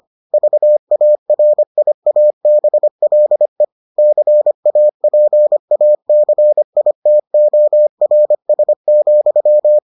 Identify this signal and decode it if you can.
Morse code, heard as a keyed tone, about 25 wpm, 600 Hz. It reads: VARIABLE CAPACITORS,